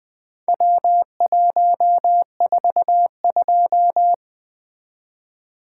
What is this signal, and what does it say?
Morse code, heard as a keyed tone, about 20 wpm, 700 Hz. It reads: W142
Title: Four Letter-Number Combinations - 20wpm